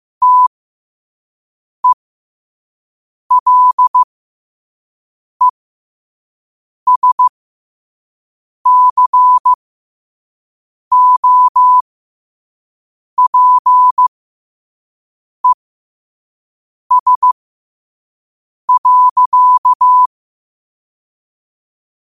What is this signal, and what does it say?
Morse code, heard as a keyed tone, about 15 wpm, spaced as for 5 wpm, 1000 Hz. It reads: TELESCOPES.
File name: qotdfiles / 5wpm